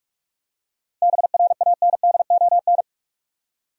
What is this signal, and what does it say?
Morse code, heard as a keyed tone, about 43 wpm, 700 Hz. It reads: BRANDON
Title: USA Common Names - 45wpm